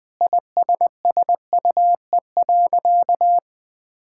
Morse code, heard as a keyed tone, about 20 wpm, 700 Hz. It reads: ISSUE.
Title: Sentences with 6-Word Phrases - Spoken Beginning - 20wpm